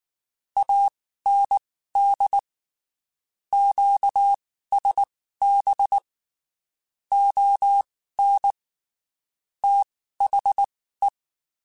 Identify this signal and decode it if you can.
Morse code, heard as a keyed tone, about 19 wpm, spaced as for 14 wpm, 800 Hz. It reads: AND QSB ON THE
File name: (spaced) / QSOs Spaced-mp3s / WD0BGP 1 @18 sp